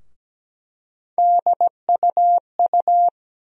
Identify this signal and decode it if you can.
Morse code, heard as a keyed tone, about 17 wpm, 700 Hz. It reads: DUU
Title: USA Call Sign Postfixes - 3 Letters 17wpm